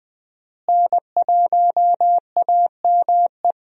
Morse code, heard as a keyed tone, about 20 wpm, 700 Hz. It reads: N1AME
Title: USA Call Signs - 5 Characters - Repeated 2x - 20wpm